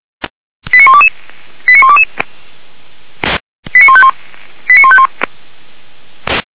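Five-tone selective call (ZVEI), heard as quick runs of five tones, tones of about 70 ms. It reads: 89122 89122 89151 89151
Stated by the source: ZVEI five-tone selective call 89345 received by SDR, from the feuerwehr-orlamuende audio server.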